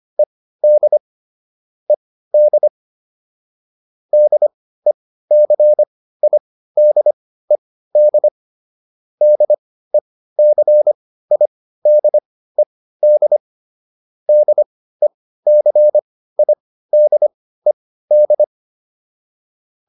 Morse code, heard as a keyed tone, about 25 wpm, spaced as for 15 wpm, 600 Hz. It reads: ED ED DECIDED DECIDED DECIDED